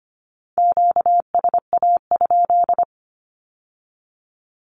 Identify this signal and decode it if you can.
Morse code, heard as a keyed tone, about 25 wpm, 700 Hz. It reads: QSA?
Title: Q-Codes with Spoken Meaning - 25wpm